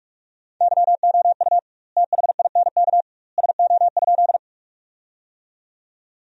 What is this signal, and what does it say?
Morse code, heard as a keyed tone, about 46 wpm, 700 Hz. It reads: YOU THINK SO?
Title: 2x Word-Spacing - Sentences from Top 200 Words - 45wpm